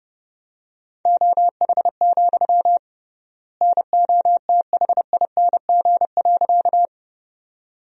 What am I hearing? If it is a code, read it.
Morse code, heard as a keyed tone, about 30 wpm, 700 Hz. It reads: OH, NOTHING.
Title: 3x Word-Spacing - Sentences from Top 500 Words - 30wpm